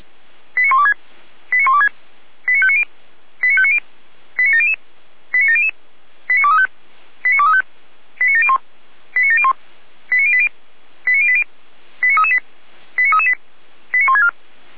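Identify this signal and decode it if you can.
Five-tone selective call (ZVEI), heard as quick runs of five tones, tones of about 80 ms. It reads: 89126 89126 89590 89590 89700 89700 89235 89235 89701 89701 89080 89080 89338 89338 89164
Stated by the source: ZVEI five-tone call received on SDR.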